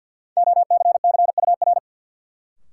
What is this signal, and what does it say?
Morse code, heard as a keyed tone, about 53 wpm, 700 Hz. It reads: OCCUR
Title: Sentences from Top 900 Words - New Word Drill - 50wpm